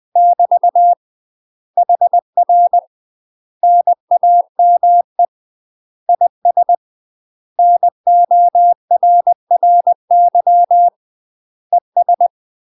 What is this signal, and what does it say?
Morse code, heard as a keyed tone, about 20 wpm, 700 Hz. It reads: <BT> HR NAME IS NORRY ES
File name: UK MASTER QSOs TRAINING / 20WPM / UK MASTER QSOs 0068